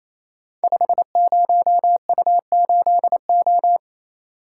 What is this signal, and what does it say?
Morse code, heard as a keyed tone, about 28 wpm, 700 Hz. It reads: 50U8O
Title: Code Groups - 5 Letter:Numbers - 28wpm